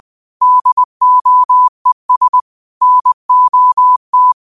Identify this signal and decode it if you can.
Morse code, heard as a keyed tone, about 20 wpm, 1000 Hz. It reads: DOES NOT
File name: qotdfiles / 20wpm